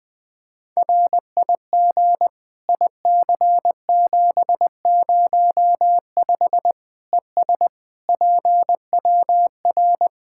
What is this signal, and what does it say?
Morse code, heard as a keyed tone, about 20 wpm, 700 Hz. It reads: RIG IC705 ES PWR
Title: Practice QSO - Exchange 2 - Info - 20wpm